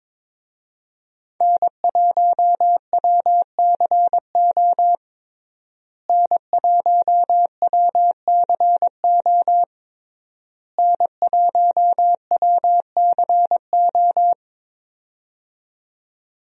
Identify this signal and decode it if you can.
Morse code, heard as a keyed tone, about 22 wpm, 700 Hz. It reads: N1WCO N1WCO N1WCO
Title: USA Call Signs - 5 Characters - Repeated 3x - 22wpm